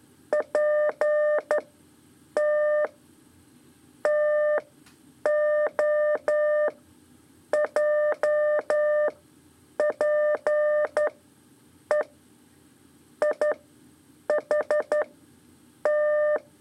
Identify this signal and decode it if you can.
Morse code, heard as a keyed tone, about 11 wpm, 600 Hz. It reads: P T T O J P E I H T
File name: 3 lezione CW 2 AIR